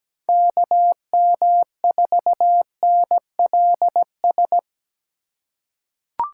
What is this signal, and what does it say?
Morse code, heard as a keyed tone, about 17 wpm, 700 Hz. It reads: KM4NLS
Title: USA Call Signs - 6 Characters - 2x Character Spacing - 17wpm